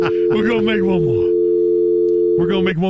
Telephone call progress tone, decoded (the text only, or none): dial tone